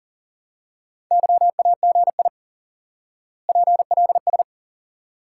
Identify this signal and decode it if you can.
Morse code, heard as a keyed tone, about 40 wpm, 700 Hz. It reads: YAGI PLS
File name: IQE Course 3 - Lesson 89 - YAGI - Two Elements - 40wpm